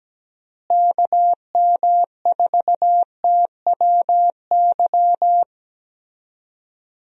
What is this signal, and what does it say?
Morse code, heard as a keyed tone, about 17 wpm, 700 Hz. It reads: KM4TWY
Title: USA Call Signs - 6 Characters - Repeated 2x - 17wpm